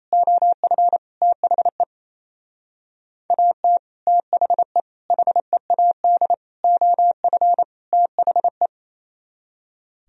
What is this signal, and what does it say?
Morse code, heard as a keyed tone, about 28 wpm, 700 Hz. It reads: OF THE AT THE HEAD OF THE
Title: Speed-Racing - Sets of 5 Words 28wpm